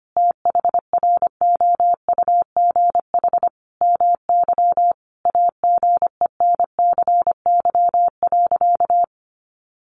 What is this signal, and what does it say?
Morse code, heard as a keyed tone, about 25 wpm, 700 Hz. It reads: THROUGH MY AGENCY.
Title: Sentences from Top 1300 Words 25wpm